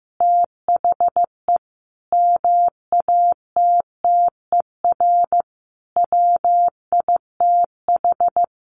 Morse code, heard as a keyed tone, about 14 wpm, 700 Hz. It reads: THE MATTER WITH
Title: Sentences from Top 700 Words 15wpm